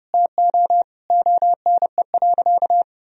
Morse code, heard as a keyed tone, about 32 wpm, 700 Hz. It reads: TO ONE.
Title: Sentences from Top 1000 Words - Review Part 3 - 30wpm